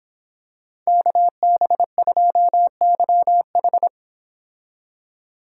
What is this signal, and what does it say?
Morse code, heard as a keyed tone, about 26 wpm, 700 Hz. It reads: KB2YH
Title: Speed Racing - USA 2x2 Call Signs 22wpm